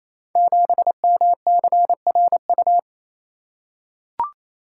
Morse code, heard as a keyed tone, about 27 wpm, 700 Hz. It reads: 7MCRU
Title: Code Groups - 5 Letter:Numbers - 28wpm